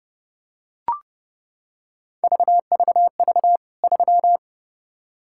Morse code, heard as a keyed tone, about 32 wpm, 700 Hz. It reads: VVV 3
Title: K1USN SST QSO Practice 30wpm